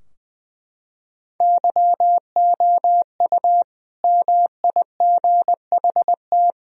Morse code, heard as a keyed tone, about 20 wpm, 700 Hz. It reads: YOU MIGHT
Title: Sentences from Top 1000 Words - Review Part 2 - 20wpm